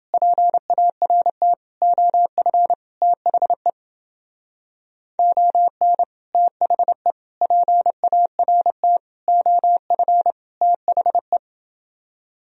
Morse code, heard as a keyed tone, about 29 wpm, 700 Hz. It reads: PART OF THE ON THE PART OF THE